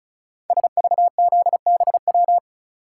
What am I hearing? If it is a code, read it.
Morse code, heard as a keyed tone, about 36 wpm, 690 Hz. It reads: SVZBW
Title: Code Groups - 5 Letters - 35wpm